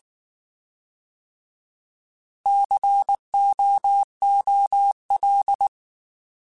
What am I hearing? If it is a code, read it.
Morse code, heard as a keyed tone, about 19 wpm, 800 Hz. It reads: COOL
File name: Alphabetical / Alphabetical mp3s / Cs @18